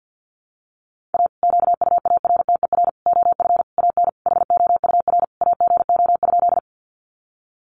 Morse code, heard as a keyed tone, about 50 wpm, 700 Hz. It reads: A QUARTER OF AN HOUR AGO?
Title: Sentences from Top 1300 Words 50wpm